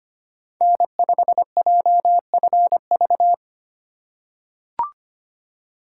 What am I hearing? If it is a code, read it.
Morse code, heard as a keyed tone, about 25 wpm, 700 Hz. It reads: N5JFV
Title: USA Call Signs - 5 Characters - 4x Character Spacing - 25wpm